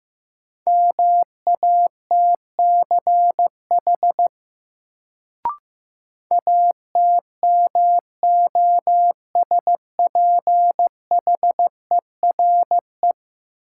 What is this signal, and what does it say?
Morse code, heard as a keyed tone, about 15 wpm, 700 Hz. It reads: MATCH ATMOSPHERE
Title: Sentences from Top 1600 Words - New Word Drill - 15wpm